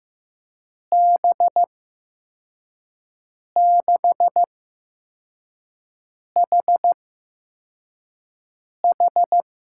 Morse code, heard as a keyed tone, about 15 wpm, spaced as for 4 wpm, 700 Hz. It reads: B6HH
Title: USA Call Signs - 8x Character Spacing - 15wpm